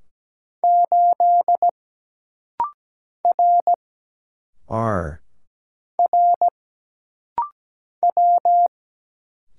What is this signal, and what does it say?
Morse code, heard as a keyed tone, about 17 wpm, 700 Hz. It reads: 8 R R W
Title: Single Letter-Number - Rapid-Fire 17wpm